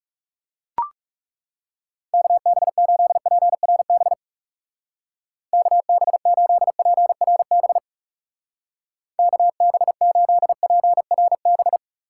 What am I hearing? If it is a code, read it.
Morse code, heard as a keyed tone, about 38 wpm, 700 Hz. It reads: KB8PRB KB8PRB KB8PRB